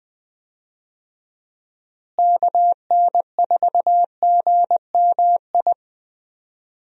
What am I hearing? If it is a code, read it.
Morse code, heard as a keyed tone, about 20 wpm, 700 Hz. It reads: KN4GMI
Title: USA Call Signs - 6 Characters - Repeated 3x - 20wpm